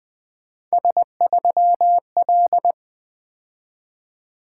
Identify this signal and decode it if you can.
Morse code, heard as a keyed tone, about 20 wpm, 700 Hz. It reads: S3L
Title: Three Letter-Number Combinations - 20wpm